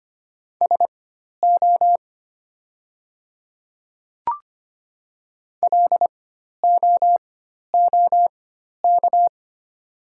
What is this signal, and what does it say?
Morse code, heard as a keyed tone, about 25 wpm, spaced as for 12 wpm, 700 Hz. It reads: SO LOOK
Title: 4x Character Spacing - Top 200 Words - 25wpm